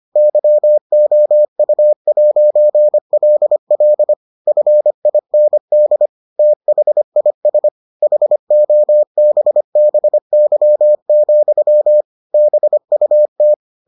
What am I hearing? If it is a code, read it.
Morse code, heard as a keyed tone, about 25 wpm, 600 Hz. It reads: YOU'LL FIND THIS HOBBY, BUT